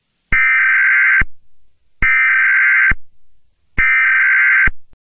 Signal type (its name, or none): Emergency Alert System (SAME) tones